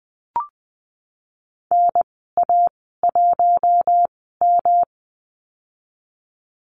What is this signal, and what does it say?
Morse code, heard as a keyed tone, about 20 wpm, spaced as for 15 wpm, 700 Hz. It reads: NA1M
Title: 173 Lesson 28g - Number 9 - Call Signs 20wpm